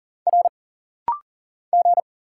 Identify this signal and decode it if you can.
Morse code, heard as a keyed tone, about 40 wpm, 700 Hz. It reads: R G
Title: Single Letter-Number - Rapid-Fire 40wpm